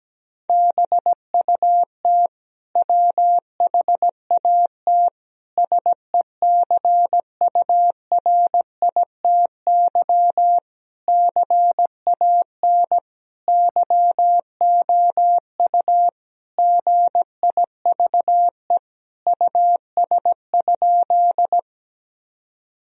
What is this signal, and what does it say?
Morse code, heard as a keyed tone, about 17 wpm, 700 Hz. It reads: BUT WHAT SECURITY CAN YOU GIVE US?
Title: Sentences from Top 1000 Words - Review Part 4 - 17wpm